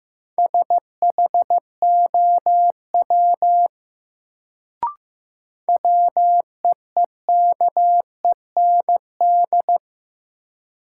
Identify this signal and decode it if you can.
Morse code, heard as a keyed tone, about 15 wpm, 700 Hz. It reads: SHOW WEEKEND
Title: Sets of 5 Words - New Word Drill - 15wpm